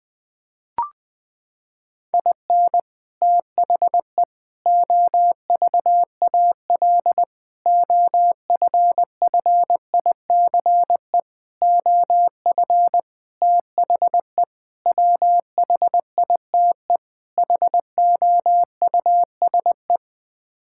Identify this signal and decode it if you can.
Morse code, heard as a keyed tone, about 20 wpm, 700 Hz. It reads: IN THE OVAL OFFICE OF THE WHITE HOUSE
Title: Sets of 8 Words - 20wpm